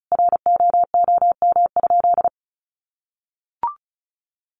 Morse code, heard as a keyed tone, about 34 wpm, 700 Hz. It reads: ROOM?
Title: Sentences from Top 1200 Words 35wpm